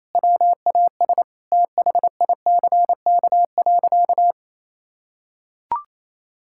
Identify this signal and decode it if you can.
Morse code, heard as a keyed tone, about 28 wpm, 700 Hz. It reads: WAS THICK.
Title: Sentences from Top 1600 Words 28wpm